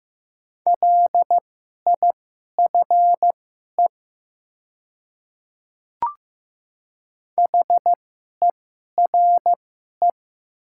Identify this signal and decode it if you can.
Morse code, heard as a keyed tone, about 15 wpm, spaced as for 11 wpm, 690 Hz. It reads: LIFE HERE
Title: 2x Character Spacing - Top 300 Words - 15wpm